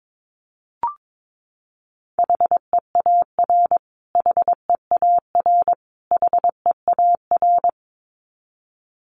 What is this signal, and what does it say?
Morse code, heard as a keyed tone, about 22 wpm, 700 Hz. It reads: HEAR HEAR HEAR
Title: Top 500 Words Repeated 22wpm